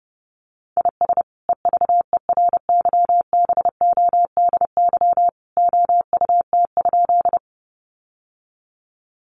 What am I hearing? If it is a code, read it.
Morse code, heard as a keyed tone, about 30 wpm, 700 Hz. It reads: IS EVERYBODY OUT?
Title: Sentences from Top 1400 Words 30wpm